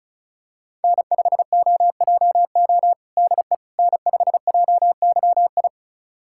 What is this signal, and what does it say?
Morse code, heard as a keyed tone, about 36 wpm, 700 Hz. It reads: N5OJO DE N5JYI
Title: USA Callsign DE Callsign - 5 Characters - 35wpm